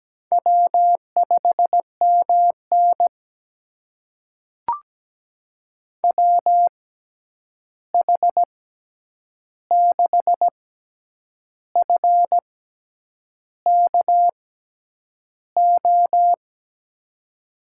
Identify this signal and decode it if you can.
Morse code, heard as a keyed tone, about 17 wpm, spaced as for 8 wpm, 700 Hz. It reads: W5MN WH6FKO
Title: USA Call Signs - 6x Character Spacing - 17wpm